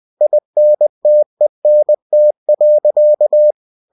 Morse code, heard as a keyed tone, about 20 wpm, 600 Hz. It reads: INTENT.
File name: cw-books / winnie-the-pooh / winnie-20wpm-page-0023